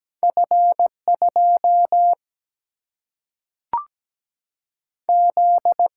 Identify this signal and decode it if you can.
Morse code, heard as a keyed tone, about 17 wpm, 700 Hz. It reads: F2 Z
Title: Single-Letter Single-Number 17wpm